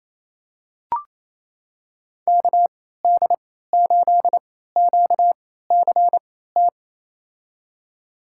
Morse code, heard as a keyed tone, about 28 wpm, spaced as for 16 wpm, 700 Hz. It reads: KD8QCT
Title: USA Call Signs - 6 Characters - 3x Character Spacing - 28wpm